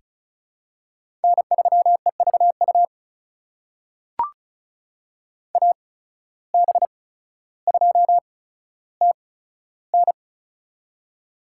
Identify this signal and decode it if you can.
Morse code, heard as a keyed tone, about 36 wpm, spaced as for 15 wpm, 700 Hz. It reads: N3EVU AB2TN
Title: USA Call Signs - 5 Characters - 8x Character Spacing - 35wpm